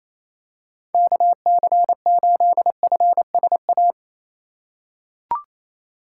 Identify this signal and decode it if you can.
Morse code, heard as a keyed tone, about 28 wpm, 700 Hz. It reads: KC8FSA
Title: USA Call Signs - 6 Characters - 3x Character Spacing - 28wpm